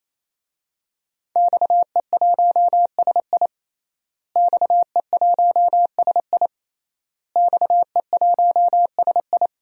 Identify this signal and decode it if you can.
Morse code, heard as a keyed tone, about 28 wpm, 700 Hz. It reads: XE1SI XE1SI XE1SI
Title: DXCC-100 Call Signs - 5 Characters - Repeated 3x - 28wpm